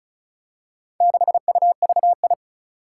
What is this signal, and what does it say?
Morse code, heard as a keyed tone, about 36 wpm, 700 Hz. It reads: 6UVI
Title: Four Challenging Letter-Number Combinations 35wpm